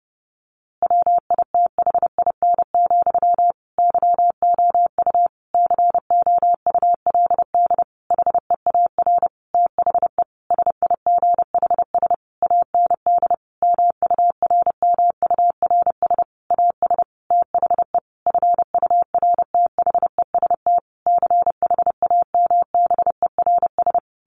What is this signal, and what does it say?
Morse code, heard as a keyed tone, about 30 wpm, 700 Hz. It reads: WITHIN, YOU COULD HEAR THE SIGHS AND MURMURS AS THE FURTHEST CHAMBERS
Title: A Sound Of Thunder 30wpm